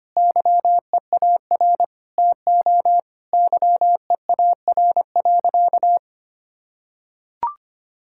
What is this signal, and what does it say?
Morse code, heard as a keyed tone, about 25 wpm, 700 Hz. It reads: YEAR TO YEAR.
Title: Sentences from Top 1600 Words 25wpm